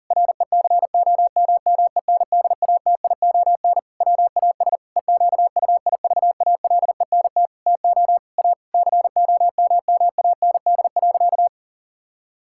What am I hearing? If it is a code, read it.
Morse code, heard as a keyed tone, about 40 wpm, 700 Hz. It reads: RECOMMENDATION WAS EQUIVALENT TO A COMMAND.
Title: Sentences from Top 1600 Words 40wpm